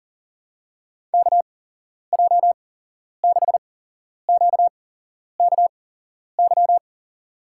Morse code, heard as a keyed tone, about 40 wpm, spaced as for 11 wpm, 690 Hz. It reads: KJ6QKY
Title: USA Call Signs - 8x Character Spacing - 40wpm